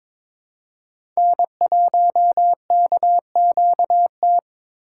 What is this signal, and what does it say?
Morse code, heard as a keyed tone, about 22 wpm, 700 Hz. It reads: N1KQT
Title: USA Call Signs - 4x Character Spacing - 22wpm